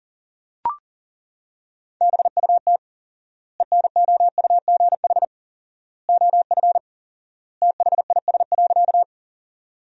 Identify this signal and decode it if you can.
Morse code, heard as a keyed tone, about 37 wpm, 700 Hz. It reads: BUT ENOUGH OF THIS.
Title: 4x Word-Spacing - Sentences from Top 400 Words - 40wpm